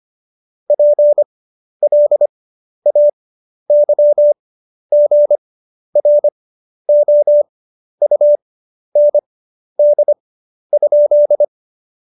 Morse code, heard as a keyed tone, about 25 wpm, spaced as for 11 wpm, 590 Hz. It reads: PLAYGROUND?
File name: Jokes-25-char-10-eff